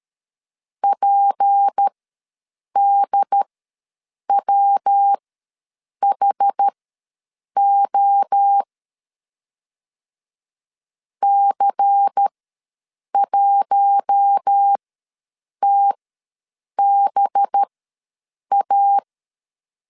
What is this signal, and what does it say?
Morse code, heard as a keyed tone, about 13 wpm, spaced as for 7 wpm, 800 Hz. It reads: PDWHO C1TBA
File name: LEZIONI / 6 PKX0 / 6 P ws 12wpm 3117